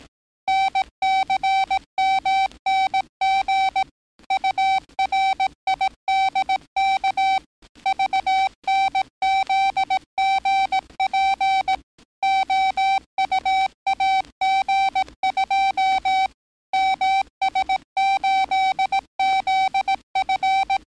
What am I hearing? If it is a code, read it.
Morse code, heard as a keyed tone, about 18 wpm, 770 Hz. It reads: NCMNG URIDK VNZGP OUAG2 MS8ZF
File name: training / NAVPERS 11049to11060 / NAVPERS 11058 RA RB